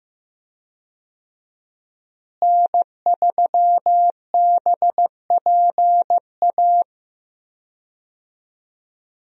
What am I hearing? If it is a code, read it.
Morse code, heard as a keyed tone, about 15 wpm, 700 Hz. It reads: N3BPA
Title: USA Call Signs - 5 Characters - Repeated 2x - 15wpm